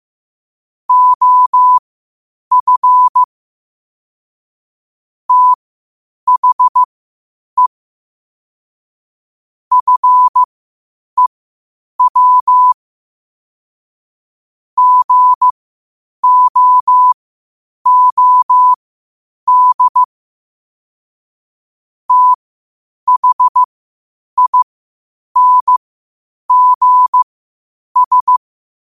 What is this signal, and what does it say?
Morse code, heard as a keyed tone, about 15 wpm, spaced as for 8 wpm, 1000 Hz. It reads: OF THE FEW GOOD THINGS
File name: qotdfiles / 8wpm